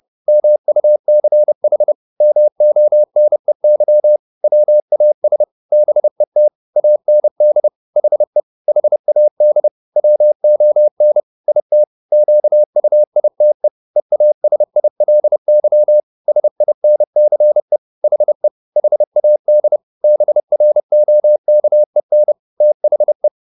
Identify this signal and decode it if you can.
Morse code, heard as a keyed tone, about 30 wpm, 600 Hz. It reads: MUCH MONEY WAS BET AND HE HAD WON IT QUITE EASILY SINCE HE HAD BROKEN THE
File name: the old man and the sea 2 30wpm